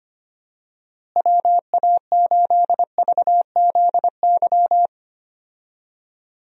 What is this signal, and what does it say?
Morse code, heard as a keyed tone, about 25 wpm, 700 Hz. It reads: WA8VZY
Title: USA Call Signs - 6 Characters - Repeated 3x - 25wpm